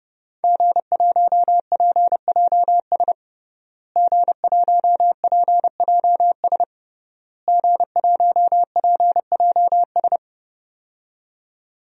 Morse code, heard as a keyed tone, about 32 wpm, 700 Hz. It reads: G1PJS G1PJS G1PJS
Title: DXCC-100 Call Signs - 5 Characters - Repeated 3x - 30wpm